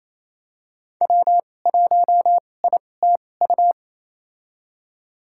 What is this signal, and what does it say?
Morse code, heard as a keyed tone, about 28 wpm, spaced as for 20 wpm, 700 Hz. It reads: W1ITU
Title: USA Call Signs - 2x Character Spacing - 28wpm